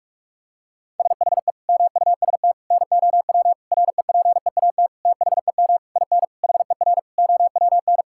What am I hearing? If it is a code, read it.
Morse code, heard as a keyed tone, about 41 wpm, 700 Hz. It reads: SHE MUST NOW REPEAT THEM IN HER OWN
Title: Sentences from Top 1600 Words 45wpm